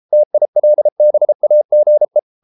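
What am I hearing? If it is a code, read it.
Morse code, heard as a keyed tone, about 33 wpm, 600 Hz. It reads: TILBAGE
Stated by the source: Morse code at 33 wpm, QRQ.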